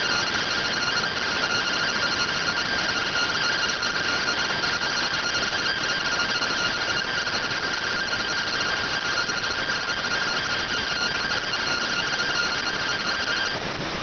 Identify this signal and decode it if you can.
Two-tone teleprinter signal, about 170 Hz shift.